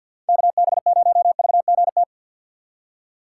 Morse code, heard as a keyed tone, about 52 wpm, 700 Hz. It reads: KB0VGT
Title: USA Call Signs - 6 Characters - 8x Character Spacing - 50wpm